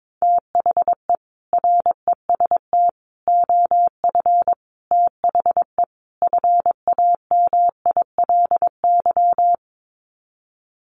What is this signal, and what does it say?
Morse code, heard as a keyed tone, about 22 wpm, 700 Hz. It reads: THE REST OF THE FAMILY
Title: Sets of 5 Words Encore2 22wpm